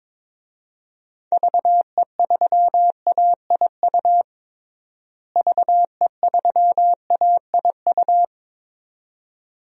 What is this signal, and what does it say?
Morse code, heard as a keyed tone, about 22 wpm, 700 Hz. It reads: VE3AIU VE3AIU
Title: DXCC-100 Call Signs - 6 Characters - Repeated 2x - 22wpm